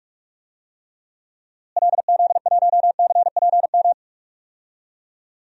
Morse code, heard as a keyed tone, about 42 wpm, 700 Hz. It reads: LZ1CPM
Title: DXCC-100 Call Signs - 6 Characters - Repeated 3x - 45wpm